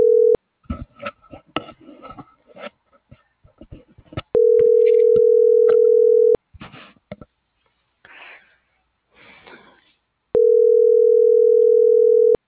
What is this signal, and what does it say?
Ringback tone (North America), 440 + 480 Hz.